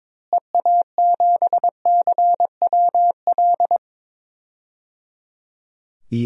Morse code, heard as a keyed tone, about 22 wpm, 700 Hz. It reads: EA7CWL
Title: DXCC-100 Call Signs - 6 Characters - Repeated 3x - 22wpm